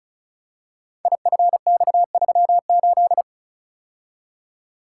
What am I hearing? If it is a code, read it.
Morse code, heard as a keyed tone, about 36 wpm, 700 Hz. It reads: IFX38
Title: Code Groups - 5 Letter:Numbers - 35wpm